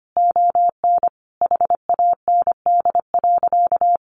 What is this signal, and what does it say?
Morse code, heard as a keyed tone, about 25 wpm, 700 Hz. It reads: ON HAND.
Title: Sentences from Top 1500 Words 25wpm